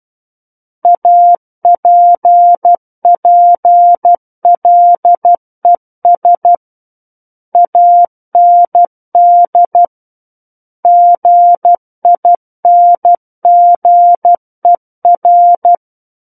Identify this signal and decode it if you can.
Morse code, heard as a keyed tone, about 12 wpm, 700 Hz. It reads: APPLES AND GINGER
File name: Book One / 03-Chapter 3 - 12 WPM